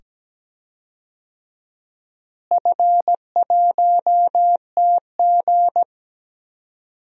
Morse code, heard as a keyed tone, about 17 wpm, 700 Hz. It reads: F1TG
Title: DXCC-100 Call Signs - 4 Characters - Repeated 2x - 17wpm